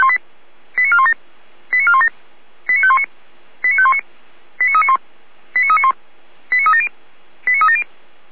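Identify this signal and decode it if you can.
Five-tone selective call (ZVEI), heard as quick runs of five tones, tones of about 80 ms. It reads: 89417 89417 89519 89519 89391 89391 89370 89370
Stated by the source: ZVEI five-tone call 89361 received by SDR, archived by fire brigade Feuerwehr Orlamünde.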